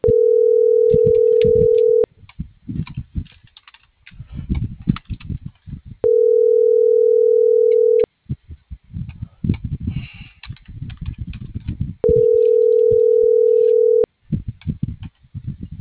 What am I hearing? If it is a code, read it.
Ringback tone (North America), 440 + 480 Hz.